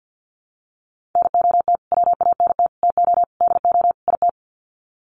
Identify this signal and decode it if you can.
Morse code, heard as a keyed tone, about 50 wpm, 700 Hz. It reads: NOT WANT TO DO IT
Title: Sets of 5 Words Encore 50wpm